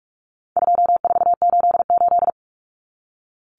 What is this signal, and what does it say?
Morse code, heard as a keyed tone, about 44 wpm, 700 Hz. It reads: 2488
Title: Numbers - 4-Digit 45wpm